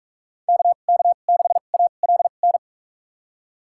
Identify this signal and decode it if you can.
Morse code, heard as a keyed tone, about 46 wpm, spaced as for 33 wpm, 700 Hz. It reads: KK6ALN
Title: USA Call Signs - 6 Characters - 2x Character Spacing - 45wpm